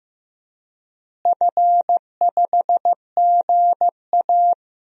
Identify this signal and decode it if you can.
Morse code, heard as a keyed tone, about 15 wpm, 700 Hz. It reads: F5GA
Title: DXCC-100 Call Signs - 4 Characters - Repeated 3x - 15wpm